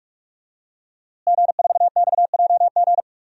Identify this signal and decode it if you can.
Morse code, heard as a keyed tone, about 44 wpm, 700 Hz. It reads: G4XJG
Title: DXCC-100 Call Signs - 5 Characters - Repeated 3x - 45wpm